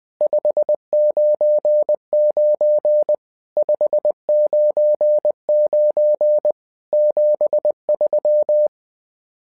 Morse code, heard as a keyed tone, about 20 wpm, 600 Hz. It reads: 599 599 73
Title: SOTA QSO Practice 20wpm